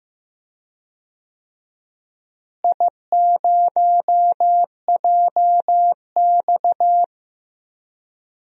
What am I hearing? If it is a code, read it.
Morse code, heard as a keyed tone, about 15 wpm, 700 Hz. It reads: I0JX
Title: DXCC-100 Call Signs - 4 Characters - Repeated 3x - 15wpm